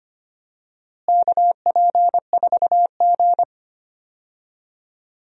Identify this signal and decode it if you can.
Morse code, heard as a keyed tone, about 25 wpm, 700 Hz. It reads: KP4G
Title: DXCC-100 Call Signs - 4 Characters - Repeated 3x - 25wpm